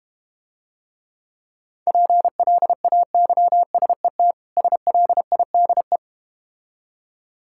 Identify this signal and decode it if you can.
Morse code, heard as a keyed tone, about 32 wpm, 700 Hz. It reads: PLAYSET SLIDE